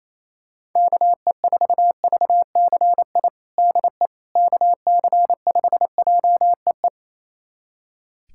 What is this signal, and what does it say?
Morse code, heard as a keyed tone, about 27 wpm, 700 Hz. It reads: KE4VCI DE KC5JEE
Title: USA Callsign DE Callsign - 6 Characters - 28wpm